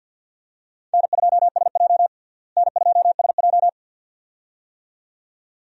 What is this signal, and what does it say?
Morse code, heard as a keyed tone, about 51 wpm, 700 Hz. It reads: N2SJ N2SJ
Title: USA Call Signs - 4 Characters - Repeated 2x - 50wpm